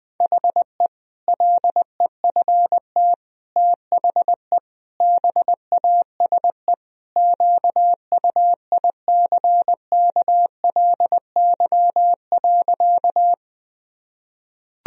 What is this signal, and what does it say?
Morse code, heard as a keyed tone, about 20 wpm, 700 Hz. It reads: HE LEFT THE BASE QUICKLY.
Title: Sentences from Top 1000 Words - Review Part 1 - 20wpm